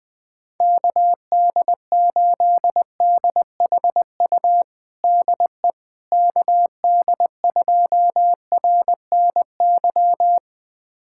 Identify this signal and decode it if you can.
Morse code, heard as a keyed tone, about 20 wpm, 700 Hz. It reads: KD8DHU DE KD2RNY
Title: USA Callsign DE Callsign 20wpm